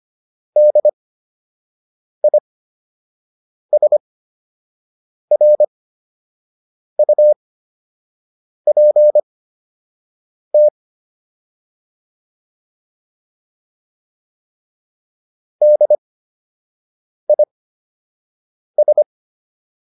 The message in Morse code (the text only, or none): DISRUPT DIS